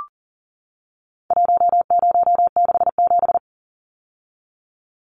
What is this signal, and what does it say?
Morse code, heard as a keyed tone, about 40 wpm, 700 Hz. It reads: 1067
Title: Numbers - 4-Digit 40wpm